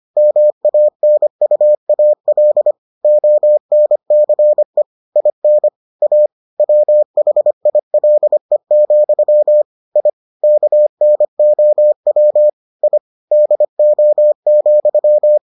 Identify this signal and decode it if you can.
Morse code, heard as a keyed tone, about 25 wpm, 600 Hz. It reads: MANUAL ONCE IN A WHILE, I KNOW I DO,